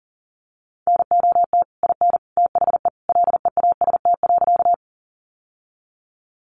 Morse code, heard as a keyed tone, about 40 wpm, 700 Hz. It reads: NOT IN THE LEAST.
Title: Sentences from Top 700 Words 40wpm